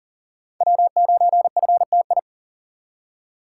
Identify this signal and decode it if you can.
Morse code, heard as a keyed tone, about 40 wpm, 700 Hz. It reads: W9FTI
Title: USA Call Signs - 2x Character Spacing - 40wpm